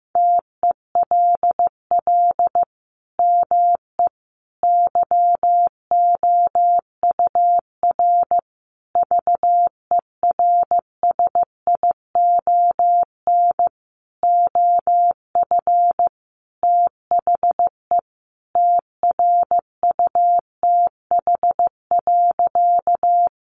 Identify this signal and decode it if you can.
Morse code, heard as a keyed tone, about 15 wpm, 700 Hz. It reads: TELL ME YOUR VERSION OF THE TRUTH.
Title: Sentences from Top 1400 Words 15wpm